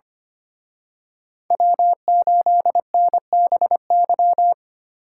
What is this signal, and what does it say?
Morse code, heard as a keyed tone, about 30 wpm, 700 Hz. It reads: W8NBY
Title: Speed Racing - USA 1x3 Call Signs 25wpm